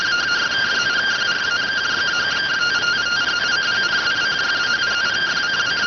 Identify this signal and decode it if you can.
Two-tone teleprinter signal, about 170 Hz shift.